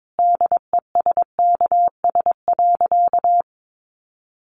Morse code, heard as a keyed tone, about 22 wpm, 700 Hz. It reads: DESKS.
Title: Additional Easy Phrases 22wpm